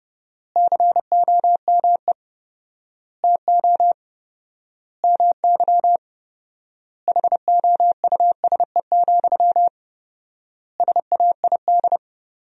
Morse code, heard as a keyed tone, about 30 wpm, 700 Hz. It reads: COME TO MY HOUSE, SAID
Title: 4x Word-Spacing - Sentences from Top 400 Words - 30wpm